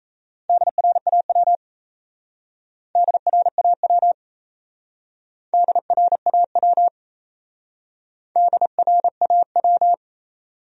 Morse code, heard as a keyed tone, about 36 wpm, 700 Hz. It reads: DRAW DRAW DRAW DRAW